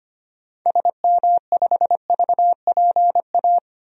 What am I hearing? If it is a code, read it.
Morse code, heard as a keyed tone, about 25 wpm, 700 Hz. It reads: SM5VPA
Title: DXCC-100 Call Signs - 6 Characters - 25wpm